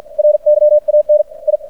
Morse code, heard as a keyed tone, about 56 wpm, spaced as for 47 wpm, 600 Hz. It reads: RYAN I
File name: archived / CRSnet